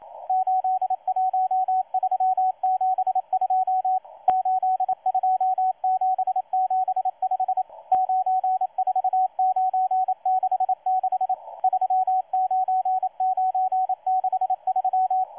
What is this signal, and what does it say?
Morse code, heard as a keyed tone, about 27 wpm, 750 Hz, with radio noise. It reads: 81372 82775 94966 39963
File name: M12-2016-07-07-1310utc-13926khz-error-restart